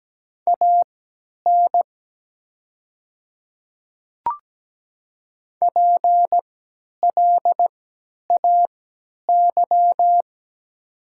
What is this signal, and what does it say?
Morse code, heard as a keyed tone, about 17 wpm, spaced as for 10 wpm, 700 Hz. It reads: AN PLAY
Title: 3x Character Spacing - Top 500 Words - 17wpm